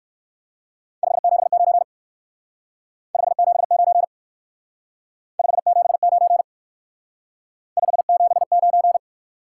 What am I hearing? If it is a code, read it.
Morse code, heard as a keyed tone, about 56 wpm, 700 Hz. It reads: 579 579 579 579
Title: Speed-Racing Top 180 QSO Elements 45wpm